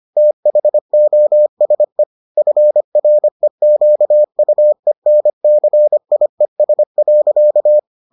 Morse code, heard as a keyed tone, about 25 wpm, 600 Hz. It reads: THOSE FREQUENCIES.